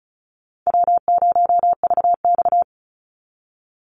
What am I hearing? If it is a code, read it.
Morse code, heard as a keyed tone, about 34 wpm, 700 Hz. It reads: W0VX
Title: Easy Call Signs Repeated 3x - 35wpm